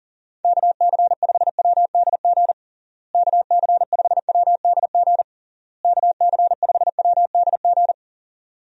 Morse code, heard as a keyed tone, about 40 wpm, 700 Hz. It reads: KC5WDG KC5WDG KC5WDG
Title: USA Call Signs - 6 Characters - Repeated 3x - 40wpm